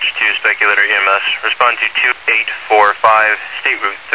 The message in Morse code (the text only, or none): none